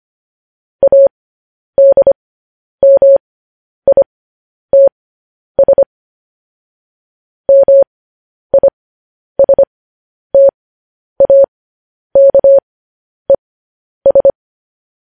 Morse code, heard as a keyed tone, about 25 wpm, spaced as for 10 wpm, 550 Hz. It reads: ADMITS MISTAKES
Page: news headlines in morse code at 10 wpm